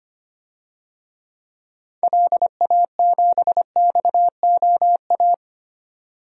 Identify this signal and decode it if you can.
Morse code, heard as a keyed tone, about 25 wpm, 700 Hz. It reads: LA7XOA
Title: DXCC-100 Call Signs - 6 Characters - 25wpm